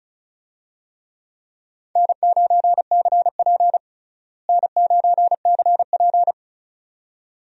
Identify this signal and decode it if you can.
Morse code, heard as a keyed tone, about 34 wpm, 700 Hz. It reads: N9CP N9CP
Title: USA Call Signs - 4 Characters - Repeated 2x - 35wpm